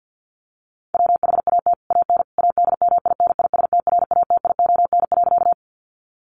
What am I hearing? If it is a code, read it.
Morse code, heard as a keyed tone, about 48 wpm, 700 Hz. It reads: WHAT AN ADMINISTRATION.
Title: Sentences from Top 1300 Words 50wpm